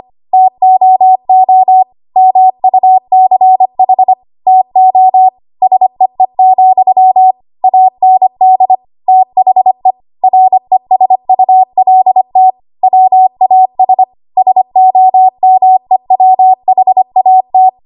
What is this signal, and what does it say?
Morse code, heard as a keyed tone, about 25 wpm, 750 Hz. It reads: TOO MUCH TO SEE, AND THE RESULT WAS SOMEWHAT
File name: Archive / 25 WPM / 25WPM